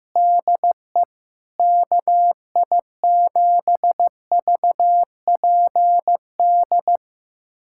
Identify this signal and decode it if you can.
Morse code, heard as a keyed tone, about 15 wpm, 700 Hz. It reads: DE KI7VPD
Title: USA Callsign DE Callsign 15wpm